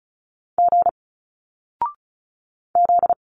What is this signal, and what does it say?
Morse code, heard as a keyed tone, about 35 wpm, 700 Hz. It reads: G Z
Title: Single Letters 35wpm